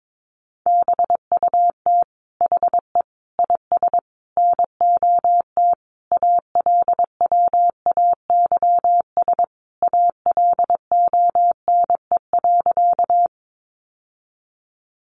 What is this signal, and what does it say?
Morse code, heard as a keyed tone, about 22 wpm, 700 Hz. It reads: BUT HE IS NOT ALWAYS ALONE.
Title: Sentences from Top 1000 Words 22wpm